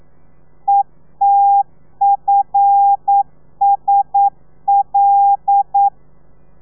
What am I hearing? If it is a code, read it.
Morse code, heard as a keyed tone, about 9 wpm, 800 Hz. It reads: ETFSL
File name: telegrafia / i2tfj / Lez5 40